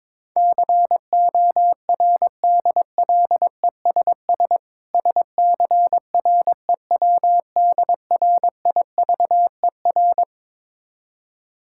Morse code, heard as a keyed tone, about 22 wpm, 690 Hz. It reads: CORDLESS SCREWDRIVER
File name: Difficult Words with Hints - 22wpm